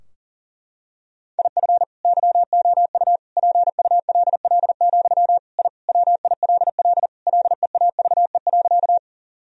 Morse code, heard as a keyed tone, about 40 wpm, 700 Hz. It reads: IF YOU PULL, I WILL LEAVE.
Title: Sentences from Top 1000 Words - Review Part 2 - 40wpm